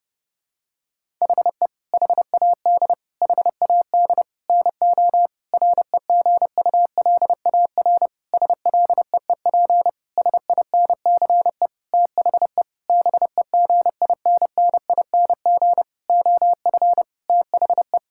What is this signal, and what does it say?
Morse code, heard as a keyed tone, about 30 wpm, 700 Hz. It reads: HE HAD HAD NO REGULAR SLEEP SINCE THE BEGINNING OF THE
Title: Sentences from Top 1000 Words - Review Part 6 - 30wpm